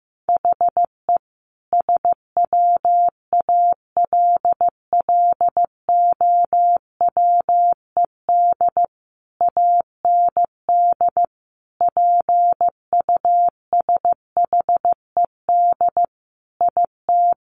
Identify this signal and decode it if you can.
Morse code, heard as a keyed tone, about 15 wpm, 700 Hz. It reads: HE SWALLOWED AND PUSHED IT
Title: A Sound Of Thunder 15wpm